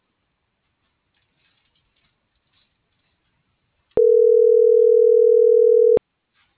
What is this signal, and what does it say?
Ringback tone (North America), 440 + 480 Hz.